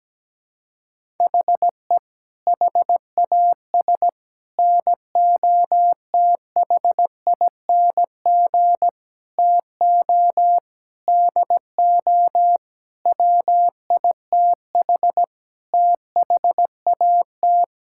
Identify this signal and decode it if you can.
Morse code, heard as a keyed tone, about 17 wpm, 700 Hz. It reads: HE HAS NOTHING TO DO WITH THAT
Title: Sentences from Top 1000 Words - Review Part 5 - 17wpm